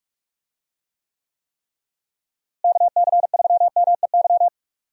Morse code, heard as a keyed tone, about 46 wpm, 700 Hz. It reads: KC3MEY
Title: USA Call Signs - 6 Characters - Repeated 3x - 45wpm